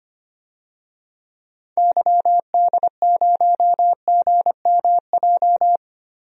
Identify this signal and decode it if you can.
Morse code, heard as a keyed tone, about 25 wpm, 700 Hz. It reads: YD0GMJ
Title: DXCC-100 Call Signs - 6 Characters - Repeated 2x - 25wpm